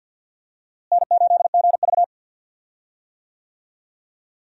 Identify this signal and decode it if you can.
Morse code, heard as a keyed tone, about 52 wpm, 700 Hz. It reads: N8GV
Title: USA Call Signs - 4 Characters - Repeated 2x - 50wpm